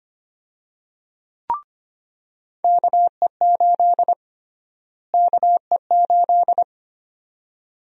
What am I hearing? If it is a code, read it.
Morse code, heard as a keyed tone, about 25 wpm, 700 Hz. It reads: KE8 KE8
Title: USA Call Sign Prefixes - Repeated 2x - 25wpm